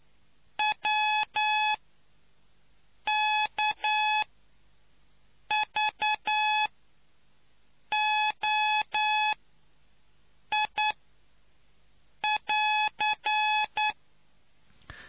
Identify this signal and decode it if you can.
Morse code, heard as a keyed tone, about 10 wpm, spaced as for 5 wpm, 850 Hz. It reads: WKVOI<AR>